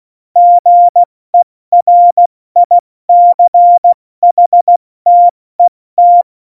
Morse code, heard as a keyed tone, about 16 wpm, spaced as for 14 wpm, 700 Hz. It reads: GERICHTET
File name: Morsekurs / Witze / Witz 15. September 2020, 70 F80